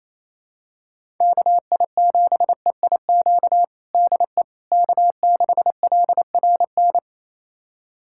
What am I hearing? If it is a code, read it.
Morse code, heard as a keyed tone, about 28 wpm, 700 Hz. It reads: KI7EIQ DE K6LRN
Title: USA Callsign DE Callsign 28wpm